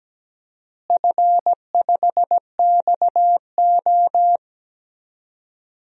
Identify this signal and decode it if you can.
Morse code, heard as a keyed tone, about 17 wpm, 700 Hz. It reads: F5XO
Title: DXCC-100 Call Signs - 4 Characters - Repeated 2x - 17wpm